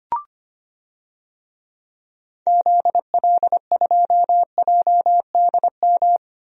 Morse code, heard as a keyed tone, about 25 wpm, 700 Hz. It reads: ZL2JDM
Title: DXCC-100 Call Signs - 6 Characters - 25wpm